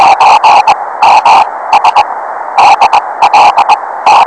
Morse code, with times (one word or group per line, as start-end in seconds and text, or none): none